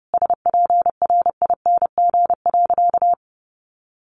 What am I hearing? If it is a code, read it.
Morse code, heard as a keyed tone, about 30 wpm, 700 Hz. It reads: SPRING.
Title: Sentences from Top 1200 Words 30wpm